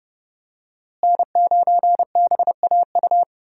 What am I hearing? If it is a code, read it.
Morse code, heard as a keyed tone, about 30 wpm, 700 Hz. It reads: N9BAU
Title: USA Call Signs - 5 Characters - Repeated 3x - 30wpm